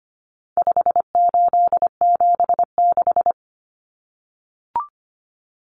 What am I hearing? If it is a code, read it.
Morse code, heard as a keyed tone, about 25 wpm, 700 Hz. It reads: 5876
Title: Numbers - 4-Digit 25wpm